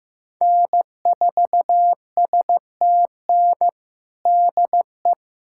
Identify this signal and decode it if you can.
Morse code, heard as a keyed tone, about 15 wpm, 700 Hz. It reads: N4STN DE
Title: USA Callsign DE Callsign 15wpm